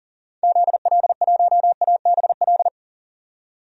Morse code, heard as a keyed tone, about 38 wpm, 700 Hz. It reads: ZL1ABL
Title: DXCC-100 Call Signs - 6 Characters - 40wpm